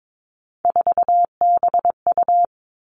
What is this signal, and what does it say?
Morse code, heard as a keyed tone, about 22 wpm, 700 Hz. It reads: 4BU
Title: Three Challenging Letter-Number Combinations 22wpm